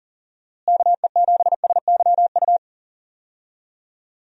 Morse code, heard as a keyed tone, about 37 wpm, 700 Hz. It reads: KE7SYU
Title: USA Call Signs - 6 Characters - Repeated 2x - 40wpm